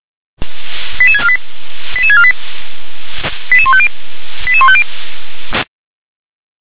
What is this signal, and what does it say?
Five-tone selective call (ZVEI), heard as quick runs of five tones, tones of about 70 ms. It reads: 88648 88648 88150 88150